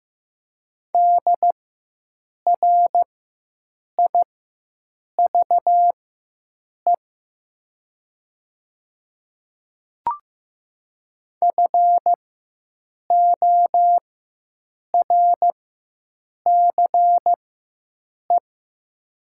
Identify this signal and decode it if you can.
Morse code, heard as a keyed tone, about 15 wpm, spaced as for 7 wpm, 690 Hz. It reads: DRIVE FORCE
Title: 4x Character Spacing - Top 500 Words - 15wpm